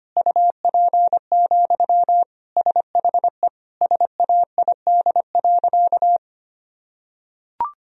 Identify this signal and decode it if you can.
Morse code, heard as a keyed tone, about 25 wpm, 700 Hz. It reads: UP, SHE SAID.
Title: Sentences from Top 1000 Words - Review Part 1 - 25wpm